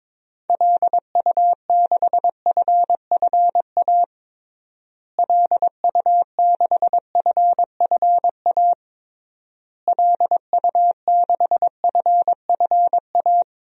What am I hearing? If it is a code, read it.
Morse code, heard as a keyed tone, about 21 wpm, 700 Hz. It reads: LU6FFA LU6FFA LU6FFA
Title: DXCC-100 Call Signs - 6 Characters - Repeated 3x - 22wpm